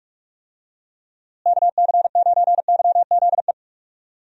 Morse code, heard as a keyed tone, about 42 wpm, 700 Hz. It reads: KC9YZE
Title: USA Call Signs - 6 Characters - Repeated 3x - 45wpm